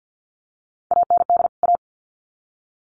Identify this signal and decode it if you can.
Morse code, heard as a keyed tone, about 50 wpm, 700 Hz. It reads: AND A
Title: Sets of 2 Words 50wpm